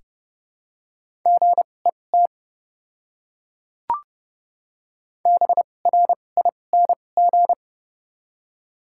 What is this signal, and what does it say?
Morse code, heard as a keyed tone, about 31 wpm, spaced as for 22 wpm, 700 Hz. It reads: GET BRING
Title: 2x Character Spacing - Top 500 Words - 30wpm